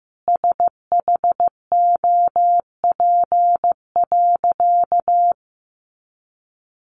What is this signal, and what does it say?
Morse code, heard as a keyed tone, about 15 wpm, 700 Hz. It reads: SHOP.
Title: Easy Phrases 15wpm